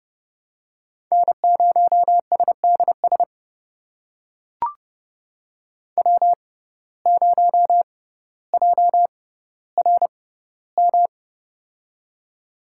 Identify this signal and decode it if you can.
Morse code, heard as a keyed tone, about 30 wpm, spaced as for 15 wpm, 700 Hz. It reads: N0SDS W0JRM
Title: USA Call Signs - 5 Characters - 6x Character Spacing - 30wpm